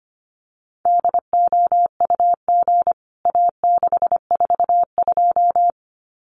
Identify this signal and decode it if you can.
Morse code, heard as a keyed tone, about 25 wpm, 700 Hz. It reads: DOUG A642
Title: CWT Exchange 25wpm